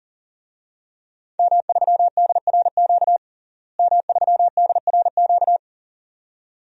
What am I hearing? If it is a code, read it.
Morse code, heard as a keyed tone, about 40 wpm, 700 Hz. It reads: M3DRQ M3DRQ
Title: DXCC-100 Call Signs - 5 Characters - Repeated 2x - 40wpm